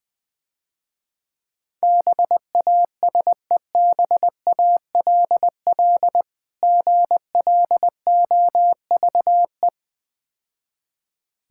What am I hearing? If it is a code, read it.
Morse code, heard as a keyed tone, about 20 wpm, 700 Hz. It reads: BASEBALL GLOVE
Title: Difficult Words - Speed-Racing with Hints - 20wpm